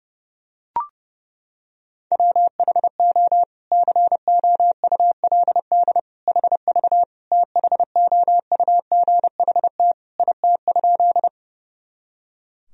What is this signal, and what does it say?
Morse code, heard as a keyed tone, about 32 wpm, 700 Hz. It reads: WHO COULD HV THOUGHT IT?
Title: Sentences from Top 300 Words and Abbreviations 30wpm